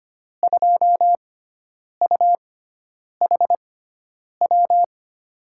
Morse code, heard as a keyed tone, about 25 wpm, spaced as for 9 wpm, 700 Hz. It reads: 2UHW
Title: USA Call Signs - 6 Characters - 6x Character Spacing - 25wpm